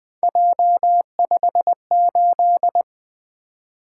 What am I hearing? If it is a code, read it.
Morse code, heard as a keyed tone, about 20 wpm, 700 Hz. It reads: J58
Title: Three Letter-Number Combinations - 20wpm